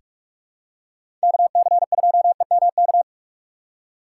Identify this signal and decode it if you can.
Morse code, heard as a keyed tone, about 45 wpm, 700 Hz. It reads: K C 2 E M K
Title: USA Call Signs - 6 Characters - 6x Character Spacing - 45wpm